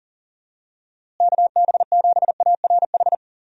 Morse code, heard as a keyed tone, about 40 wpm, 700 Hz. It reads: KB7ARH
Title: USA Call Signs - 6 Characters - Repeated 3x - 40wpm